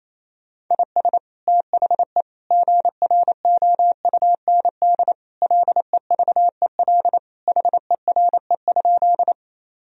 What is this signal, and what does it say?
Morse code, heard as a keyed tone, about 27 wpm, 700 Hz. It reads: IS THE GROUND LEVEL HERE?
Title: Sentences from Top 1000 Words - Review Part 4 - 28wpm